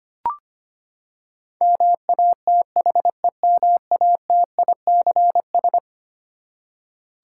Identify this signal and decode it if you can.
Morse code, heard as a keyed tone, about 25 wpm, 700 Hz. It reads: MATHEMATICS
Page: morse-code-ninja-practice-25wpm